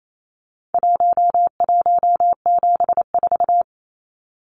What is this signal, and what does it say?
Morse code, heard as a keyed tone, about 28 wpm, 700 Hz. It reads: 1174
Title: Numbers - Contest Style - 4-Digit 28wpm